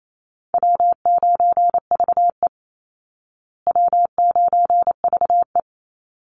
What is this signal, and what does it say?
Morse code, heard as a keyed tone, about 28 wpm, 700 Hz. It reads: W9VE W9VE
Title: Easy Call Signs Repeated 2x - 28wpm